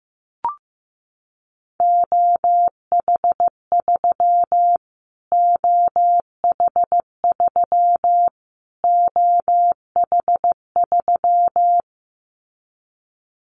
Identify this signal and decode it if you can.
Morse code, heard as a keyed tone, about 15 wpm, 700 Hz. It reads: OH3 OH3 OH3
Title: Callsign Prefixes for Top 25 DXCC - Repeated 3x - 15wpm